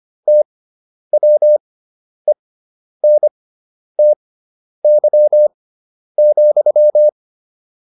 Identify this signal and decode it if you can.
Morse code, heard as a keyed tone, about 25 wpm, spaced as for 10 wpm, 600 Hz. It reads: TWENTY,